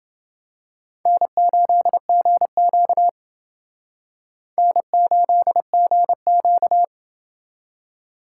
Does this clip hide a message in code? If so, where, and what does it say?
Yes: Morse code from 1.1 to 6.8 s, heard as a keyed tone, about 29 wpm, 700 Hz. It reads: N8GQ N8GQ